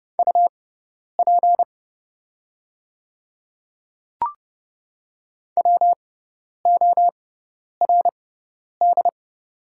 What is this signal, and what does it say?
Morse code, heard as a keyed tone, about 31 wpm, spaced as for 10 wpm, 700 Hz. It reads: UP WORD
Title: 6x Character Spacing - Top 100 Words - 30wpm